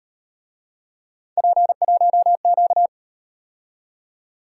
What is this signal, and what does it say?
Morse code, heard as a keyed tone, about 37 wpm, 700 Hz. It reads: P1Q